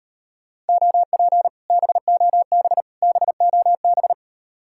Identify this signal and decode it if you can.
Morse code, heard as a keyed tone, about 38 wpm, 700 Hz. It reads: OP BOB BOB